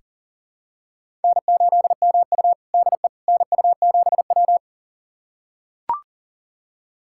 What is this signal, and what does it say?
Morse code, heard as a keyed tone, about 40 wpm, 700 Hz. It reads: N8MU DE NU7W
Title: USA Callsign DE Callsign - 4 Characters - 40wpm